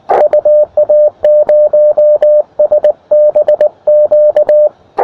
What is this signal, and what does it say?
Morse code, heard as a keyed tone, about 18 wpm, 600 Hz, with radio noise. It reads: UA0SBQ
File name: attachments / ua0sbq 20m